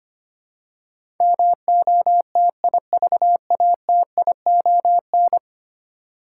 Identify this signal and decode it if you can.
Morse code, heard as a keyed tone, about 25 wpm, 700 Hz. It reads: MOTIVATION
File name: Ten Letter Words - 25wpm